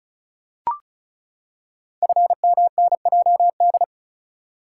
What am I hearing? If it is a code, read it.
Morse code, heard as a keyed tone, about 36 wpm, 700 Hz. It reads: FMNJD
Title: Code Groups - 5 Letters - 35wpm